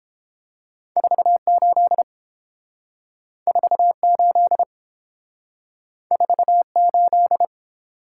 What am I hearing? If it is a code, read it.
Morse code, heard as a keyed tone, about 28 wpm, 700 Hz. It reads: 48 48 48